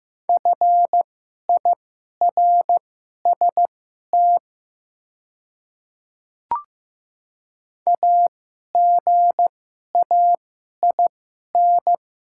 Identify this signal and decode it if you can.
Morse code, heard as a keyed tone, about 15 wpm, spaced as for 11 wpm, 690 Hz. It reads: FIRST AGAIN
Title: 2x Character Spacing - Top 500 Words - 15wpm